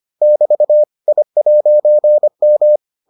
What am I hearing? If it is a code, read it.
Morse code, heard as a keyed tone, about 25 wpm, 600 Hz. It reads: <BT> I'M